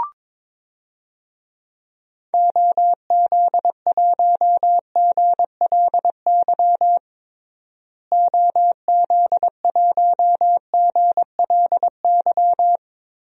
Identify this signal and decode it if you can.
Morse code, heard as a keyed tone, about 21 wpm, 700 Hz. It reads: OZ1GLY OZ1GLY
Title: DXCC-100 Call Signs - 6 Characters - Repeated 2x - 22wpm